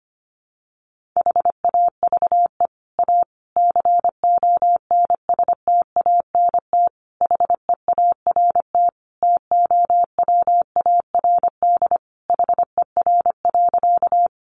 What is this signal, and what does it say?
Morse code, heard as a keyed tone, about 25 wpm, 700 Hz. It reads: HAVE A CONSTANT HEART TOWARD HER.
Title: Sentences from Top 1500 Words 25wpm